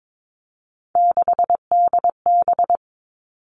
Morse code, heard as a keyed tone, about 22 wpm, 700 Hz. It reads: 6DB
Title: Three Challenging Letter-Number Combinations 22wpm